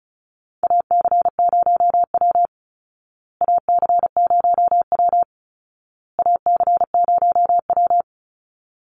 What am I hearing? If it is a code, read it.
Morse code, heard as a keyed tone, about 34 wpm, 700 Hz. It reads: AC0W AC0W AC0W
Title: Call Signs Repeated 2x - 35wpm